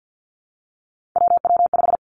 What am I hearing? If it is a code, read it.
Morse code, heard as a keyed tone, about 50 wpm, 700 Hz. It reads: WW5
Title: Three Challenging Letter-Number Combinations 50wpm